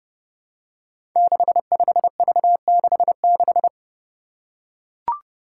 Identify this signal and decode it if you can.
Morse code, heard as a keyed tone, about 30 wpm, 700 Hz. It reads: 65V66
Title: Five Challenging Letter-Number Combinations 30wpm